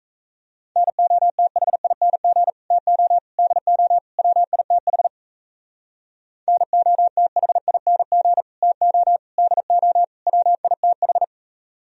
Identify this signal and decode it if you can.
Morse code, heard as a keyed tone, about 35 wpm, 700 Hz. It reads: NOTHING TO DO WITH NOTHING TO DO WITH